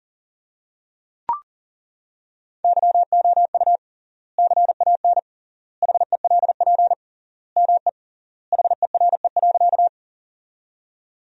Morse code, heard as a keyed tone, about 40 wpm, 700 Hz. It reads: YOU CAN HELP ME HERE.
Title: 3x Word-Spacing - Sentences from Top 200 Words - 40wpm